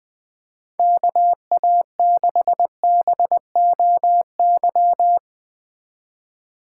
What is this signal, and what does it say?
Morse code, heard as a keyed tone, about 20 wpm, 700 Hz. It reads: KA6BOY